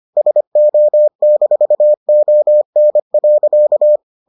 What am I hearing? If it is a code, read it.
Morse code, heard as a keyed tone, about 25 wpm, 600 Hz. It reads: SO-ON.